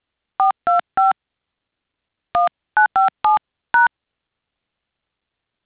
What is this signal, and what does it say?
DTMF, heard as telephone touch tones, tones of about 150 ms. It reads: 425 1957#
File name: things / india intercept